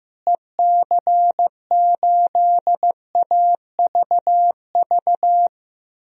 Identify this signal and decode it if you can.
Morse code, heard as a keyed tone, about 16 wpm, 700 Hz. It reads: EC8AVV
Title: DXCC-100 Call Signs - 6 Characters - Repeated 3x - 15wpm